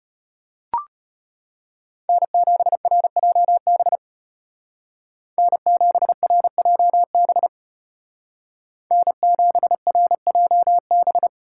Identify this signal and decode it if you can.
Morse code, heard as a keyed tone, about 33 wpm, 700 Hz. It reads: N7RJB N7RJB N7RJB